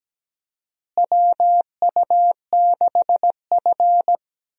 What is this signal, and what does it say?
Morse code, heard as a keyed tone, about 17 wpm, 700 Hz. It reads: WU6F
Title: USA Call Signs - 4 Characters - Repeated 3x - 17wpm